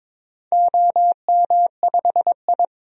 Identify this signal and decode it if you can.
Morse code, heard as a keyed tone, about 22 wpm, 700 Hz. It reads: OM5I
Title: DXCC-100 Call Signs - 4 Characters - 22wpm